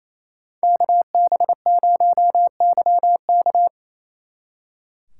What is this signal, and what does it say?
Morse code, heard as a keyed tone, about 34 wpm, 700 Hz. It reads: KB0YK
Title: Speed Racing - USA 2x2 Call Signs 28wpm